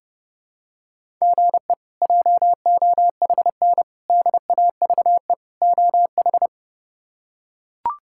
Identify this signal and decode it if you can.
Morse code, heard as a keyed tone, about 30 wpm, 700 Hz. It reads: GE JOHN DAVE OH
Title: K1USN SST QSO Practice - 30wpm